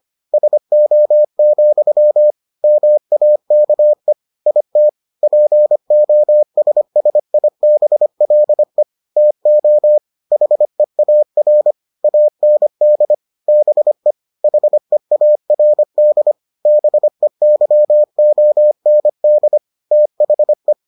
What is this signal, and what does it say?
Morse code, heard as a keyed tone, about 25 wpm, 600 Hz. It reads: SO, MAKE IT POSSIBLE TO HEAR AND BE HEARD BEYOND THE